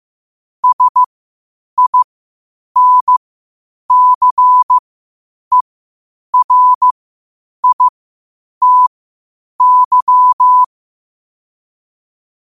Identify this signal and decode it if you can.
Morse code, heard as a keyed tone, about 15 wpm, spaced as for 8 wpm, 1000 Hz. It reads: SINCERITY
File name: qotdfiles / 8wpm